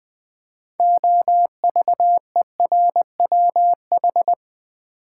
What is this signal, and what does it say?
Morse code, heard as a keyed tone, about 20 wpm, 700 Hz. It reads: OVERWH
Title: Build A Word - 3 Letters At A Time - 9-Letter Words - 20wpm